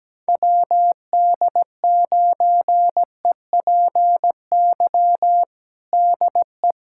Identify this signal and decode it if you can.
Morse code, heard as a keyed tone, about 17 wpm, 700 Hz. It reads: WD9EPY DE
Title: USA Callsign DE Callsign 17wpm